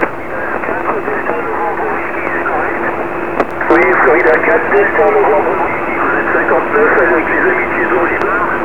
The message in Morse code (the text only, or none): none